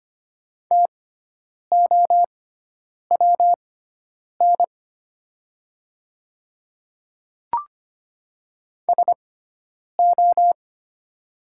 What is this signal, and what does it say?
Morse code, heard as a keyed tone, about 25 wpm, spaced as for 9 wpm, 700 Hz. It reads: TOWN SO
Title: 6x Character Spacing - Top 500 Words - 25wpm